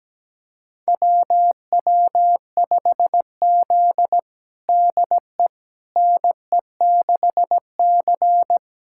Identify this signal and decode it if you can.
Morse code, heard as a keyed tone, about 17 wpm, 700 Hz. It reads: WW5Z DE NE6C
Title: USA Callsign DE Callsign - 4 Characters - 17wpm